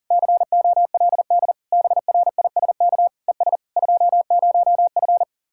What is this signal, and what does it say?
Morse code, heard as a keyed tone, about 50 wpm, 700 Hz. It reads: COLD BRISK ES 20F